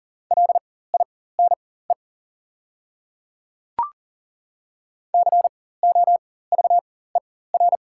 Morse code, heard as a keyed tone, about 40 wpm, spaced as for 19 wpm, 700 Hz. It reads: LINE COVER
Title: 4x Character Spacing - Top 300 Words - 40wpm